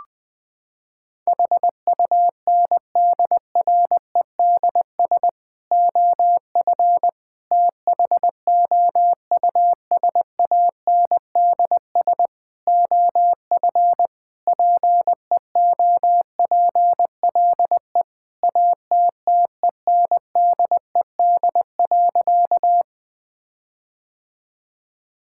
Morse code, heard as a keyed tone, about 21 wpm, 700 Hz. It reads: HUNDREDS OF THOUSANDS OF PEOPLE ATTENDED.
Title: Sentences with 4-Word Phrases - 20wpm